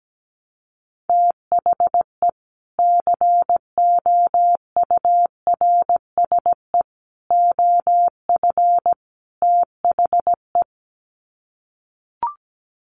Morse code, heard as a keyed tone, about 17 wpm, 700 Hz. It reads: THE COURSE OF THE
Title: Sets of 4 Words 17wpm